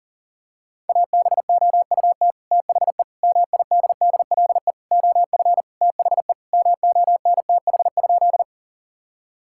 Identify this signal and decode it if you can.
Morse code, heard as a keyed tone, about 40 wpm, 700 Hz. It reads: ABOUT THE MIDDLE OF THE MONTH?
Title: Sentences from Top 1000 Words - Review Part 6 - 40wpm